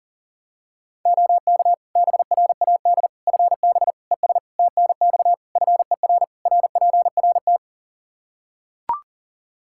Morse code, heard as a keyed tone, about 40 wpm, 700 Hz. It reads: OK BRAD FB ES TNX FER RPRT
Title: K1USN SST QSO Practice 40wpm